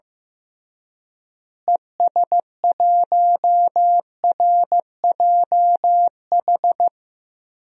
Morse code, heard as a keyed tone, about 15 wpm, 700 Hz. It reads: ES1RJH
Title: DXCC-100 Call Signs - 6 Characters - Repeated 2x - 15wpm